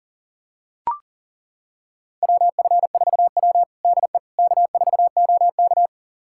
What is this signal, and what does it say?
Morse code, heard as a keyed tone, about 40 wpm, 700 Hz. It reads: WF4W DE K4OK
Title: USA Callsign DE Callsign - 4 Characters - 40wpm